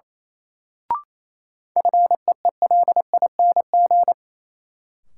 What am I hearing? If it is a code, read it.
Morse code, heard as a keyed tone, about 28 wpm, 700 Hz. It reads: FEELING
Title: Sentences from Top 1200 Words - New Word Drill Alt - 28wpm